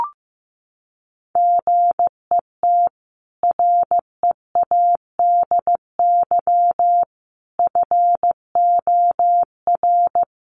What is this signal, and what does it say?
Morse code, heard as a keyed tone, about 15 wpm, 700 Hz. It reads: GET READY FOR
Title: Sentences from Top 1200 Words 15wpm